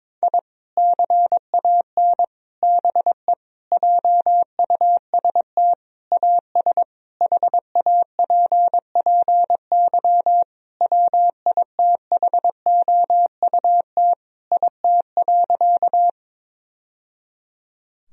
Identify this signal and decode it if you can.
Morse code, heard as a keyed tone, about 22 wpm, 700 Hz. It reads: I CAN BE JUST AS HAPPY WITHOUT IT.
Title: Sentences from Top 1000 Words - Review Part 3 - 22wpm